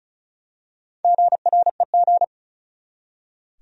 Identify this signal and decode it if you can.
Morse code, heard as a keyed tone, about 35 wpm, 700 Hz. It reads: GREG
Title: USA Common Names - Max 5 Letters - 35wpm